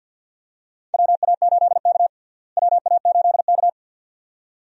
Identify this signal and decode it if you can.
Morse code, heard as a keyed tone, about 46 wpm, 700 Hz. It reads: WA8K WA8K
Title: USA Call Signs - 4 Characters - Repeated 2x - 50wpm